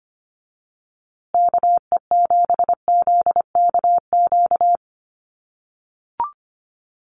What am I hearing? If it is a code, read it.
Morse code, heard as a keyed tone, about 25 wpm, 700 Hz. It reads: KE7ZKQ
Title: USA Call Signs Repeated 2x 25wpm